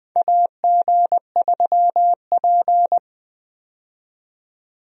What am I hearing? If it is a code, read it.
Morse code, heard as a keyed tone, about 20 wpm, 700 Hz. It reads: AG3P
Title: USA Call Signs - 4 Characters - 20wpm